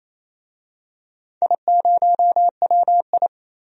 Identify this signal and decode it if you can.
Morse code, heard as a keyed tone, about 28 wpm, 700 Hz. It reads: I0WI
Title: DXCC-100 Call Signs - 4 Characters - 28wpm